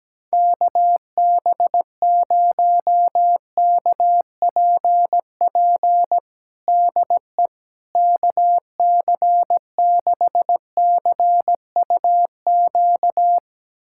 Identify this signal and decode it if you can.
Morse code, heard as a keyed tone, about 17 wpm, 700 Hz. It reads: KB0KPP DE KC6CUQ
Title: USA Callsign DE Callsign 17wpm